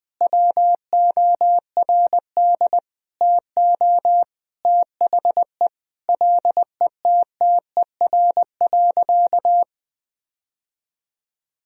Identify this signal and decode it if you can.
Morse code, heard as a keyed tone, about 20 wpm, 700 Hz. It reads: WORD TO THE LETTER.
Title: Sentences from Top 1000 Words - Review Part 1 - 20wpm